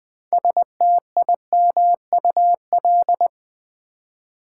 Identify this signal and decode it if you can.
Morse code, heard as a keyed tone, about 20 wpm, 700 Hz. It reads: STIMUL
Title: Build A Word - 3 Letters At A Time - 8-Letter Words - 20wpm